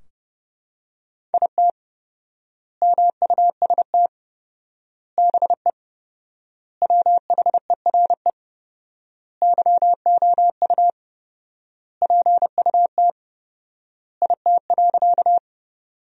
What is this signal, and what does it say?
Morse code, heard as a keyed tone, about 30 wpm, 700 Hz. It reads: IT MUST BE WHERE YOU PUT IT.
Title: 4x Word-Spacing - Sentences from Top 200 Words - 30wpm